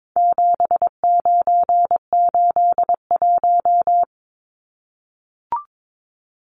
Morse code, heard as a keyed tone, about 22 wpm, 700 Hz. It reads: 7981
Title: Numbers - 4-Digit 22wpm